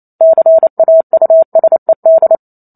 Morse code, heard as a keyed tone, about 28 wpm, 650 Hz. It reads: CAUSED